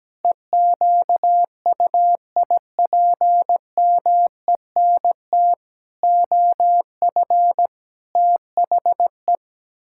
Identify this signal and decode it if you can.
Morse code, heard as a keyed tone, about 17 wpm, 700 Hz. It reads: EQUIPMENT OF THE
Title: Pangram Challenge v2 - 17wpm